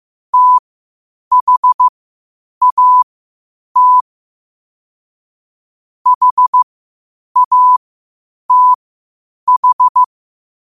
Morse code, heard as a keyed tone, about 15 wpm, spaced as for 8 wpm, 1000 Hz. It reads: THAT HATH
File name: qotdfiles / 8wpm